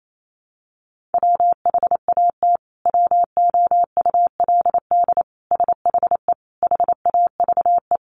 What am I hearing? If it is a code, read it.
Morse code, heard as a keyed tone, about 28 wpm, 700 Hz. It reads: WHAT WOULD SHE HAVE
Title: Sentences from Top 100 Words 28wpm